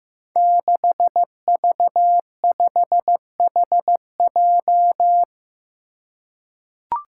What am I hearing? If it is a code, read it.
Morse code, heard as a keyed tone, about 15 wpm, 700 Hz. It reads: 6V5HJ
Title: Five Challenging Letter-Number Combinations 15wpm